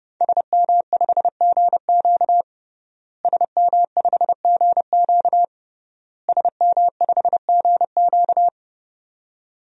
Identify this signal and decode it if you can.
Morse code, heard as a keyed tone, about 30 wpm, 700 Hz. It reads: SM5GQ SM5GQ SM5GQ
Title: DXCC-100 Call Signs - 5 Characters - Repeated 3x - 30wpm